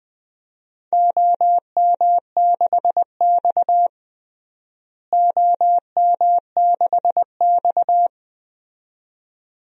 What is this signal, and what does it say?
Morse code, heard as a keyed tone, about 20 wpm, 700 Hz. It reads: OM6X OM6X
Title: DXCC-100 Call Signs - 4 Characters - Repeated 2x - 20wpm